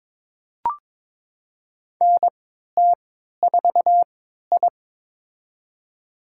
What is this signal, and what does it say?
Morse code, heard as a keyed tone, about 22 wpm, spaced as for 13 wpm, 700 Hz. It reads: NT4I
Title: USA Call Signs - 3x Character Spacing - 22wpm